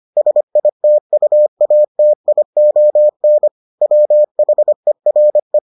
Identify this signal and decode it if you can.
Morse code, heard as a keyed tone, about 25 wpm, 600 Hz. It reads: SITUATION WHERE